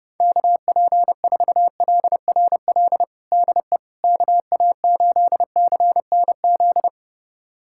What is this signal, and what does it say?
Morse code, heard as a keyed tone, about 30 wpm, 700 Hz. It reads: KP4LRL DE KA8CNZ
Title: USA Callsign DE Callsign 30wpm